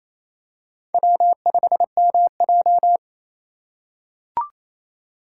Morse code, heard as a keyed tone, about 28 wpm, 700 Hz. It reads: W5MJ
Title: USA Call Signs - 4 Characters - 3x Character Spacing - 28wpm